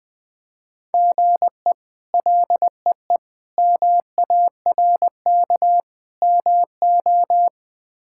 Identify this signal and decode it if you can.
Morse code, heard as a keyed tone, about 21 wpm, 700 Hz. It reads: GE LEE MARK MO
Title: K1USN SST QSO Practice - 20wpm